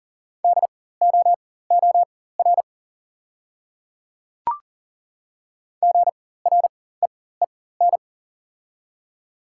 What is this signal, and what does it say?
Morse code, heard as a keyed tone, about 40 wpm, spaced as for 19 wpm, 700 Hz. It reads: DOOR GREEN
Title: 4x Character Spacing - Top 500 Words - 40wpm